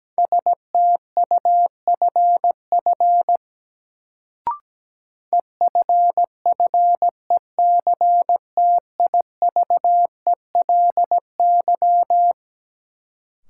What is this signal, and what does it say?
Morse code, heard as a keyed tone, about 17 wpm, 700 Hz. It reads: STUFF EFFECTIVELY
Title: Sentences from Top 1400 Words - New Word Drill - 17wpm